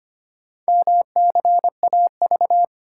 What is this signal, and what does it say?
Morse code, heard as a keyed tone, about 25 wpm, 700 Hz. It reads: MCAV
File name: practice / 25wpm / Four Letter-Number Combinations - 25wpm